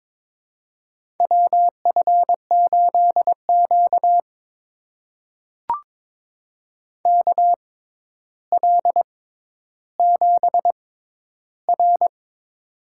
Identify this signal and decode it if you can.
Morse code, heard as a keyed tone, about 23 wpm, spaced as for 11 wpm, 700 Hz. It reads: WF8Q KL7R
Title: USA Call Signs - 4 Characters - 6x Character Spacing - 22wpm